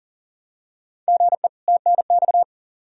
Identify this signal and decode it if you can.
Morse code, heard as a keyed tone, about 38 wpm, 700 Hz. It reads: GE TNX
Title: Practice QSO - Exchange 1 - Beginning - 40wpm